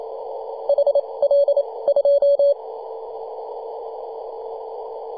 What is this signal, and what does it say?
Morse code, heard as a keyed tone, about 28 wpm, spaced as for 22 wpm, 570 Hz, with radio noise. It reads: HL 2
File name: 2015/May/15 2222z PW0F 21.026MHz CW